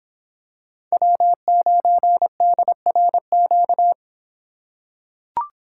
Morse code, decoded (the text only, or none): W9DRQ